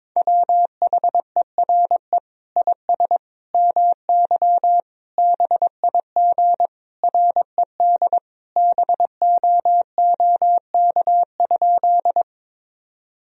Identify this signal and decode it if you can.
Morse code, heard as a keyed tone, about 21 wpm, 700 Hz. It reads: WHERE IS MY BIG RED BOOK?
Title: Sentences from Top 1000 Words - Review Part 2 - 22wpm